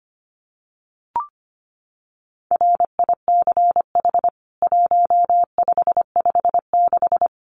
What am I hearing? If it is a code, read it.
Morse code, heard as a keyed tone, about 25 wpm, 700 Hz. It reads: RICH 1556
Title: CWT Exchange 25wpm